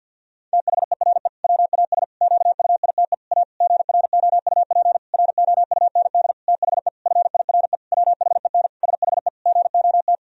Morse code, heard as a keyed tone, about 51 wpm, 700 Hz. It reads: THERE WAS QUITE A GROUP ROUND THE FIRE WHEN SHE GOT
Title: Sentences from Top 1000 Words - Review Part 1 - 50wpm